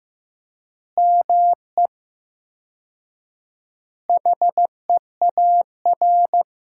Morse code, heard as a keyed tone, about 15 wpm, 700 Hz. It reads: ME HEAR
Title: 4x Word-Spacing - Sentences from Top 300 Words - 15wpm